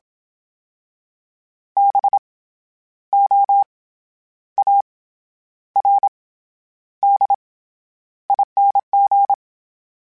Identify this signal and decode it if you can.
Morse code, heard as a keyed tone, about 27 wpm, 800 Hz. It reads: B O A R D ING
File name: Special INGs-mp3s / Special INGs-mp3s / SlowToFaster ing @25